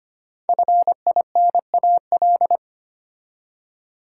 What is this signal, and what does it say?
Morse code, heard as a keyed tone, about 25 wpm, 700 Hz. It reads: FINAL
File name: 5 Letter Words - First 3 Letters Given - 25wpm